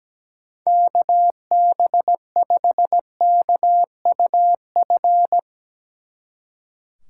Morse code, heard as a keyed tone, about 18 wpm, 700 Hz. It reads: KB5KUF
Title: Speed Racing - USA 2x3 Call Signs 17wpm